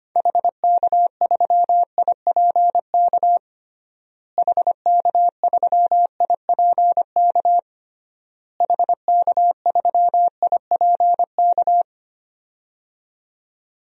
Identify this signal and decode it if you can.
Morse code, heard as a keyed tone, about 25 wpm, 690 Hz. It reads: HK3IPK HK3IPK HK3IPK
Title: DXCC-100 Call Signs - 6 Characters - Repeated 3x - 25wpm